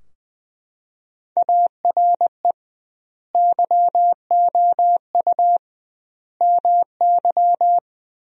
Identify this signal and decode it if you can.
Morse code, heard as a keyed tone, about 20 wpm, 700 Hz. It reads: ARE YOU MY
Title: 2x Word-Spacing - Sentences from Top 300 Words - 20wpm